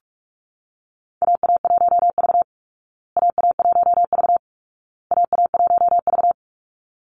Morse code, heard as a keyed tone, about 44 wpm, 700 Hz. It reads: AA1V AA1V AA1V
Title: Call Signs Repeated 2x - 45wpm